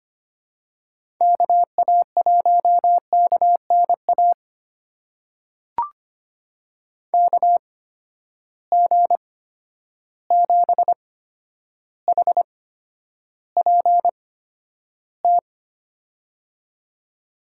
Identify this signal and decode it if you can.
Morse code, heard as a keyed tone, about 25 wpm, spaced as for 11 wpm, 700 Hz. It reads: KA1KNA KG7HPT
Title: USA Call Signs - 6 Characters - 8x Character Spacing - 25wpm